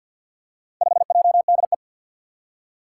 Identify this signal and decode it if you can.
Morse code, heard as a keyed tone, about 52 wpm, 700 Hz. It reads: 5JDE
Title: Four Challenging Letter-Number Combinations 50wpm